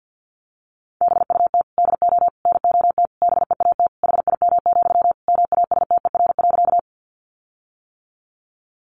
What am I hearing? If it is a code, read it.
Morse code, heard as a keyed tone, about 50 wpm, 700 Hz. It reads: BUT DO NOT BEAT HIM, MASTER.
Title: Sentences from Top 1300 Words 50wpm